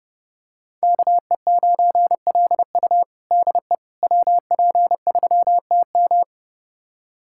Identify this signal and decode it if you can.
Morse code, heard as a keyed tone, about 30 wpm, 700 Hz. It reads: KE9LU DE WP3TM
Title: USA Callsign DE Callsign - 5 Characters - 30wpm